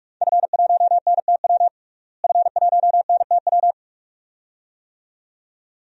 Morse code, heard as a keyed tone, about 46 wpm, 700 Hz. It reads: F1NTW F1NTW
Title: DXCC-100 Call Signs - 5 Characters - Repeated 2x - 45wpm